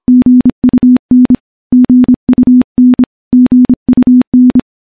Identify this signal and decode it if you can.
Morse code, heard as a keyed tone, about 26 wpm, 250 Hz. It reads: GUN GUN GUN
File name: morse-training-25WPM